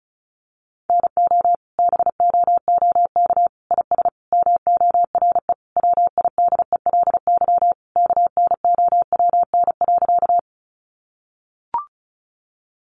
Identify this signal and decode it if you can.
Morse code, heard as a keyed tone, about 34 wpm, 700 Hz. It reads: NO BOOK IS MORE WIDELY KNOWN.
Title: Sentences from Top 1200 Words 35wpm